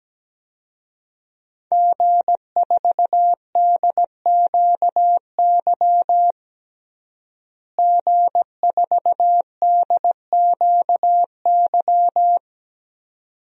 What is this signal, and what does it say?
Morse code, heard as a keyed tone, about 17 wpm, 700 Hz. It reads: G4DQY G4DQY
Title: DXCC-100 Call Signs - 5 Characters - Repeated 2x - 17wpm